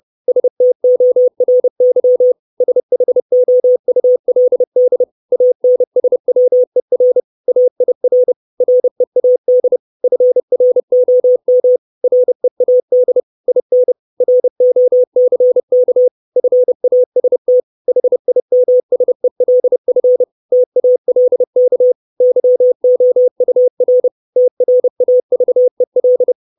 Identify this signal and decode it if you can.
Morse code, heard as a keyed tone, about 30 wpm, 500 Hz. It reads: STORY SHOULD ANSWER AIR READ FROM READ IN ROCK FAST HIMSELF TALK YOUR TRAVEL
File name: 632 Words 30WPM